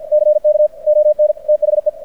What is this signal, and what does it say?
Morse code, heard as a keyed tone, about 50 wpm, 590 Hz. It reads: CK ON THE
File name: archived / CRSnet